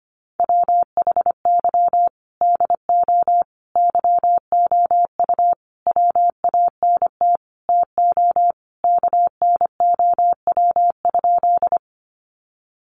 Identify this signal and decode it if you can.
Morse code, heard as a keyed tone, about 25 wpm, 700 Hz. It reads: WHY DO YOU WANT TO KNOW?
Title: Sentences from Top 200 Words 25wpm